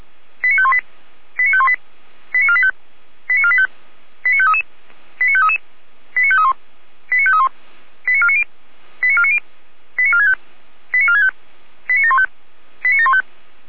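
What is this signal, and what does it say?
Five-tone selective call (ZVEI), heard as quick runs of five tones, tones of about 70 ms. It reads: 89518 89518 89485 89485 89533 89533 89531 89531 89490 89490 89465 89465 89715 89715
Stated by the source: ZVEI five-tone selective call received by SDR, from audio.f85360.feuerwehr-orlamuende.org, a fire brigade (Feuerwehr Orlamünde) server.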